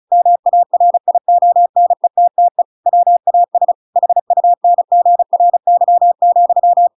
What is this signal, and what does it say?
Morse code, heard as a keyed tone, about 34 wpm, 700 Hz. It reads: MARIONETTE WAS HUNGRY,
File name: Pinocchio / Pinocchio ch7 35 35 ik1hge